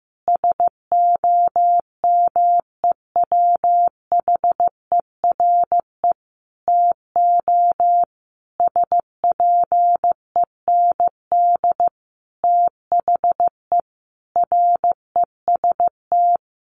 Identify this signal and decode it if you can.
Morse code, heard as a keyed tone, about 15 wpm, 700 Hz. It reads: SOMEWHERE TO SPEND THE REST
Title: Sentences from Top 1400 Words 15wpm